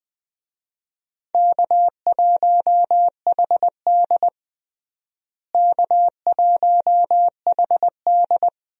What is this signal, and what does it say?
Morse code, heard as a keyed tone, about 20 wpm, 700 Hz. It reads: K1HD K1HD
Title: USA Call Signs - 4 Characters - Repeated 3x - 20wpm